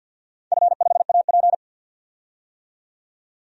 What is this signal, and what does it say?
Morse code, heard as a keyed tone, about 47 wpm, 700 Hz. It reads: F5AP
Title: DXCC-100 Call Signs - 4 Characters - 50wpm